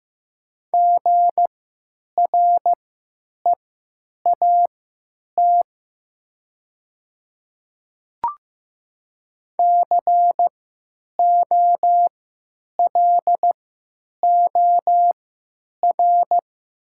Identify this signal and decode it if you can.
Morse code, heard as a keyed tone, about 15 wpm, spaced as for 9 wpm, 700 Hz. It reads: GREAT COLOR
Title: 3x Character Spacing - Top 500 Words - 15wpm